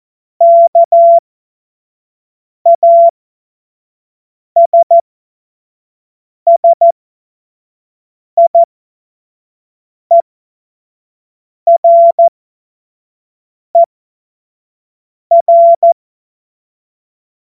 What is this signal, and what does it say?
Morse code, heard as a keyed tone, about 14 wpm, spaced as for 5 wpm, 680 Hz. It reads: KASSIERER
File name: Morsekurs / Witze / Witz 02. Mai 2023 25 F70